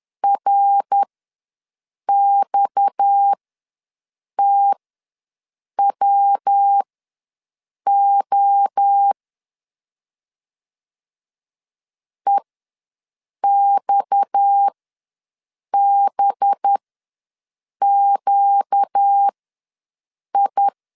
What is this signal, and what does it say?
Morse code, heard as a keyed tone, about 10 wpm, spaced as for 6 wpm, 800 Hz. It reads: RXTWO EXBQI
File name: LEZIONI / 6 PKX0 / 6 PKX ws 10wpm 3706